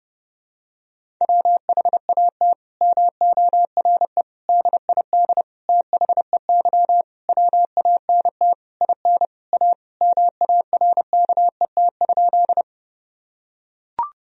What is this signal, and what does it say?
Morse code, heard as a keyed tone, about 32 wpm, 700 Hz. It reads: WHAT MORE DID THEY WANT IN A MARKET?
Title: Sentences from Top 1000 Words - Review Part 3 - 30wpm